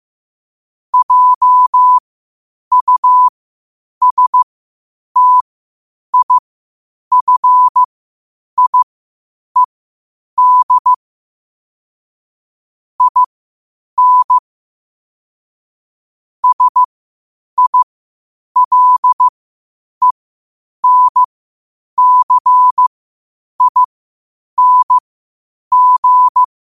Morse code, heard as a keyed tone, about 15 wpm, spaced as for 8 wpm, 1000 Hz. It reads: JUSTIFIED IN SILENCING
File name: qotdfiles / 8wpm